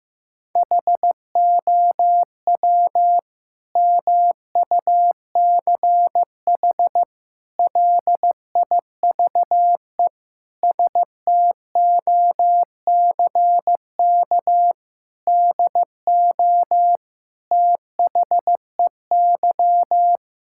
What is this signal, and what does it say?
Morse code, heard as a keyed tone, about 16 wpm, 700 Hz. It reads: HOW MUCH LIVE STOCK DO THEY
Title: Sentences from Top 1000 Words - Review Part 5 - 15wpm